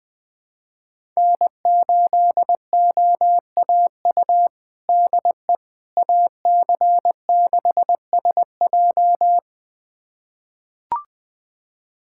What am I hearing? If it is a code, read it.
Morse code, heard as a keyed tone, about 21 wpm, 700 Hz. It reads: N8OAU DE AC6SJ
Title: USA Callsign DE Callsign - 5 Characters - 20wpm